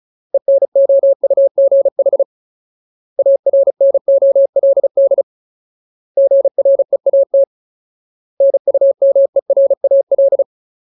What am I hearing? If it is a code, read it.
Morse code, heard as a keyed tone, about 34 wpm, 550 Hz. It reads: ENOUGH ARNOLD GREAT NUMERAL
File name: Ultimate Word List / 35WPM Ultimate